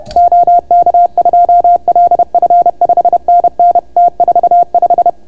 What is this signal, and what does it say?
Morse code, heard as a keyed tone, about 30 wpm, 690 Hz. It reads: OK2LF 5NN T45